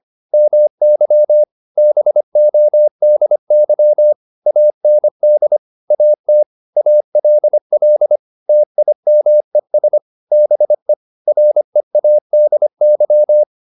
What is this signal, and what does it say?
Morse code, heard as a keyed tone, about 25 wpm, 600 Hz. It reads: MY BODY AND AT ALL TIMES BE READY